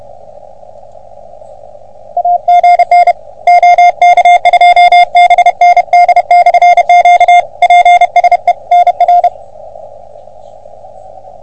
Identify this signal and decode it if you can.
Morse code, heard as a keyed tone, about 32 wpm, 720 Hz. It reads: AGN OK2BND/Q PSE NR
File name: OM3PA PILUP CUC 388